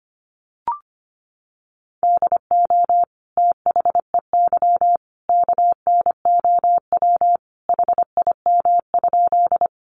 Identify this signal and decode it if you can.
Morse code, heard as a keyed tone, about 25 wpm, 700 Hz. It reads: DO THEY KNOW HIM?
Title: Sentences from Top 100 Words 25wpm